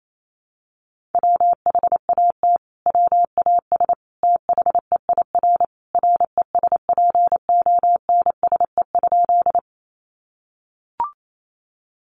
Morse code, heard as a keyed tone, about 28 wpm, 700 Hz. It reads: WHAT WAS THEIR RESPONSE?
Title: Sentences from Top 1400 Words 28wpm